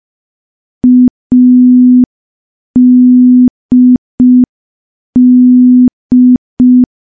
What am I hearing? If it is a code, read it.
Morse code, heard as a keyed tone, about 5 wpm, 250 Hz. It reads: ADD
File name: morse / morse-training-05WPM